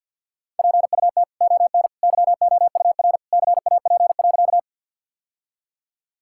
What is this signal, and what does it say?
Morse code, heard as a keyed tone, about 48 wpm, 700 Hz. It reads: PUT ON YOUR CAP.
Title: Sentences from Top 1600 Words 50wpm